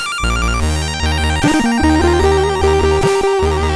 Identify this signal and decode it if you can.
There is no code.